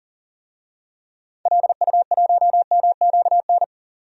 Morse code, heard as a keyed tone, about 39 wpm, 700 Hz. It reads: LU1MQN
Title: DXCC-100 Call Signs - 6 Characters - 40wpm